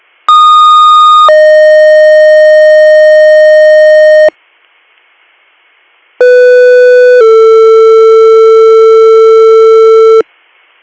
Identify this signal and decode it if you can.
Two-tone page, heard as a one-second tone, then a three-second tone, about 1230 then 620 Hz, about 500 then 450 Hz. It is heard from 0.3 to 10.2 s.